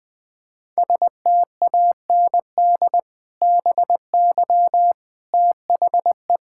Morse code, heard as a keyed tone, about 21 wpm, 700 Hz. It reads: STAND BY THE
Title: Sentences with 2-Word Phrases - 20wpm